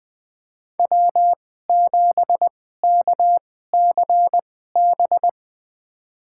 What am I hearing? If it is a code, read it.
Morse code, heard as a keyed tone, about 20 wpm, spaced as for 14 wpm, 700 Hz. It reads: W7KCB
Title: USA Call Signs - 5 Characters - 2x Character Spacing - 20wpm